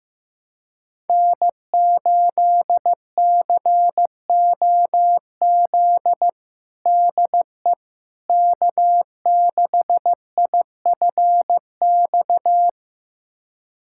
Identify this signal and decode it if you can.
Morse code, heard as a keyed tone, about 15 wpm, 700 Hz. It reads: N8COZ DE K6IFX
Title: USA Callsign DE Callsign - 5 Characters - 15wpm